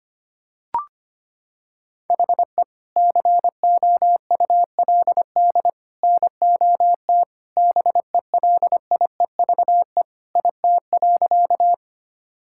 Morse code, heard as a keyed tone, about 25 wpm, 700 Hz. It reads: HE COULD NOT BELIEVE IT.
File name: practice / 25wpm / Sentences with 3-Word Phrases - 25wpm